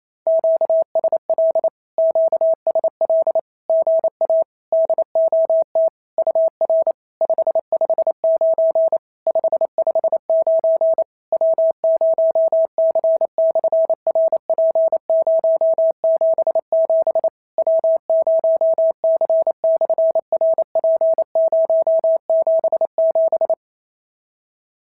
Morse code, heard as a keyed tone, about 28 wpm, 650 Hz. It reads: QSL QSL GA DOT UR 559 559 W0C/RP077 W0C/RP077
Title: SOTA QSO Practice 28wpm